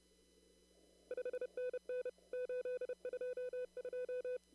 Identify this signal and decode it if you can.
Morse code, heard as a keyed tone, about 30 wpm, 500 Hz. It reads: 5NN 822
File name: CW / kt5j wpx cw-op-k5pi